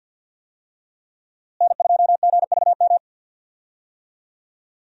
Morse code, heard as a keyed tone, about 50 wpm, 700 Hz. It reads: N2GVM
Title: USA Call Signs - 5 Characters - Repeated 2x - 50wpm